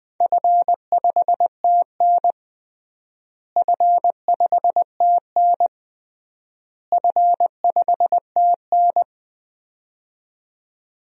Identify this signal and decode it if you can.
Morse code, heard as a keyed tone, about 20 wpm, 700 Hz. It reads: F5TN F5TN F5TN
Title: DXCC-100 Call Signs - 4 Characters - Repeated 3x - 20wpm